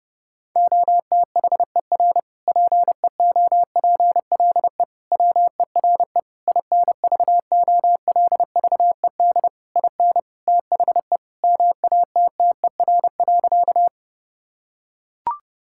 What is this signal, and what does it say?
Morse code, heard as a keyed tone, about 30 wpm, 700 Hz. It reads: OTHER PEOPLE WERE INVOLVED IN THE MATTER.
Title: Sentences from Top 1000 Words - Review Part 5 - 30wpm